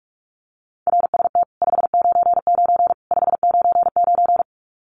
Morse code, heard as a keyed tone, about 44 wpm, 700 Hz. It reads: RST 599 599
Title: Common QSO Phrases 45wpm